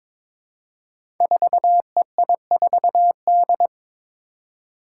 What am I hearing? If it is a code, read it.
Morse code, heard as a keyed tone, about 22 wpm, 700 Hz. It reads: 4EI4D
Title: Five Challenging Letter-Number Combinations 22wpm